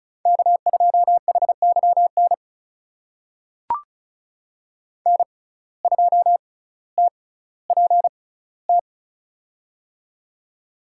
Morse code, heard as a keyed tone, about 34 wpm, spaced as for 18 wpm, 700 Hz. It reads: K2HYN N2TPT
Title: USA Call Signs - 5 Characters - 6x Character Spacing - 35wpm